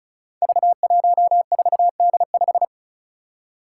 Morse code, heard as a keyed tone, about 36 wpm, 700 Hz. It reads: V14D5
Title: Five Challenging Letter-Number Combinations 35wpm